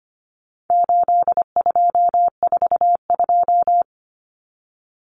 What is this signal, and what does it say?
Morse code, heard as a keyed tone, about 25 wpm, 700 Hz. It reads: 8242
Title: Common Cut Numbers - 4-Digit 25wpm